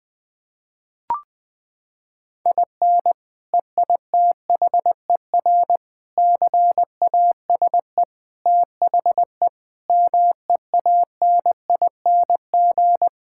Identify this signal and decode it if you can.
Morse code, heard as a keyed tone, about 20 wpm, 700 Hz. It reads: IN EITHER CASE THE MEANING
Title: Sentences from Top 1000 Words - Review Part 6 - 20wpm